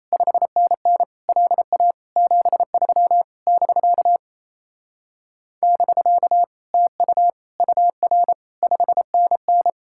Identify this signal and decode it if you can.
Morse code, heard as a keyed tone, about 38 wpm, 700 Hz. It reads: 5NN LA 73 <BK> <BK> TU UR 5NN